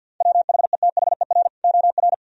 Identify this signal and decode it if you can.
Morse code, heard as a keyed tone, about 49 wpm, 700 Hz. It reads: WHETHER OR
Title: Speed-Racing - Sets of 3 Words 50wpm